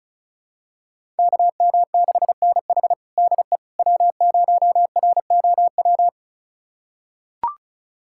Morse code, heard as a keyed tone, about 36 wpm, 700 Hz. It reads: KM6NH DE W0ROW
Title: USA Callsign DE Callsign - 5 Characters - 35wpm